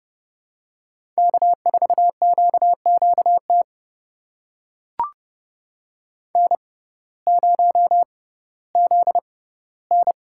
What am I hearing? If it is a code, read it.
Morse code, heard as a keyed tone, about 31 wpm, spaced as for 17 wpm, 700 Hz. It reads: K4QQT N0ZN
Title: USA Call Signs - 5 Characters - 6x Character Spacing - 30wpm